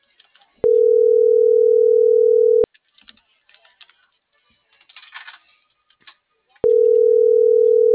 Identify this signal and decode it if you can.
Ringback tone (North America), 440 + 480 Hz.